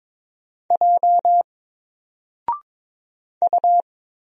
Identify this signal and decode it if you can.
Morse code, heard as a keyed tone, about 22 wpm, 700 Hz. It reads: J U
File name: Single-Letter-Number-22wpm